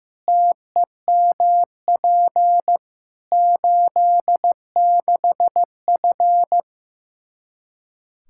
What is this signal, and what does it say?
Morse code, heard as a keyed tone, about 15 wpm, 700 Hz. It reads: TEMP 86F
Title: K1USN SST QSO Practice 15wpm